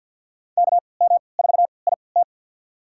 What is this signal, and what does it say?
Morse code, heard as a keyed tone, about 50 wpm, spaced as for 28 wpm, 700 Hz. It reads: KM4IT
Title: USA Call Signs - 3x Character Spacing - 50wpm